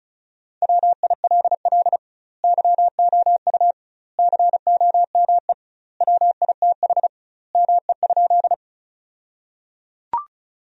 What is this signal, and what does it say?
Morse code, heard as a keyed tone, about 34 wpm, 700 Hz. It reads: WILL YOU COME WITH ME?
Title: 2x Word-Spacing - Sentences from Top 200 Words - 35wpm